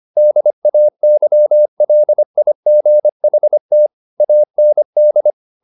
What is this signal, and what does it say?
Morse code, heard as a keyed tone, about 25 wpm, 600 Hz. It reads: DAYLIGHT AND
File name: the old man and the sea part 1 25wpm